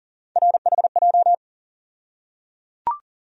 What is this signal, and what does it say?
Morse code, heard as a keyed tone, about 38 wpm, 700 Hz. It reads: RHJ
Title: USA Call Sign Postfixes - 3 Letters 40wpm